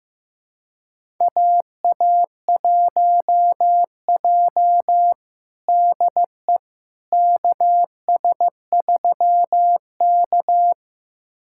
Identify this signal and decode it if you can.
Morse code, heard as a keyed tone, about 16 wpm, 700 Hz. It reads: AA1J DE KS3K
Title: USA Callsign DE Callsign - 4 Characters - 15wpm